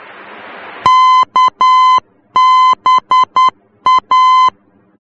Morse code, heard as a keyed tone, about 10 wpm, 1020 Hz. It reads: KBA
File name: germany / mx / KBA-D